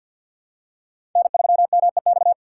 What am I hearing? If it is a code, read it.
Morse code, heard as a keyed tone, about 53 wpm, 700 Hz. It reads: N3MEX
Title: USA Call Signs - 5 Characters - 6x Character Spacing - 50wpm